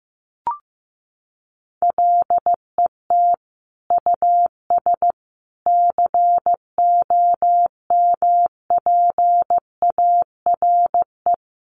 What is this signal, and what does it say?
Morse code, heard as a keyed tone, about 15 wpm, 700 Hz. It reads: LET US COMPARE
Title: Sentences from Top 1300 Words 15wpm